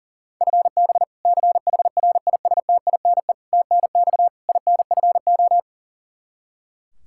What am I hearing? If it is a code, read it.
Morse code, heard as a keyed tone, about 40 wpm, 700 Hz. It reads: FB CHRISTINE TNX INFO
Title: Practice QSO - Exchange 3 - More Info - 40wpm